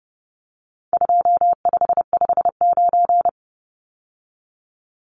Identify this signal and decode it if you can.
Morse code, heard as a keyed tone, about 30 wpm, 700 Hz. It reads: 2559
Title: Numbers - 4-Digit 30wpm